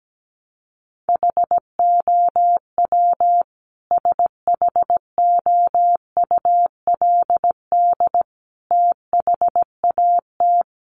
Morse code, heard as a keyed tone, about 17 wpm, 700 Hz. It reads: HOW SHOULD THAT
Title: Sentences from Top 1300 Words 17wpm